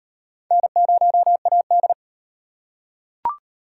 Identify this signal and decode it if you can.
Morse code, heard as a keyed tone, about 33 wpm, 700 Hz. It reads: N0AD